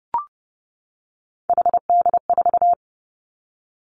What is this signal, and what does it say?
Morse code, heard as a keyed tone, about 30 wpm, 700 Hz. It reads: HD4
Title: Three Challenging Letter-Number Combinations 30wpm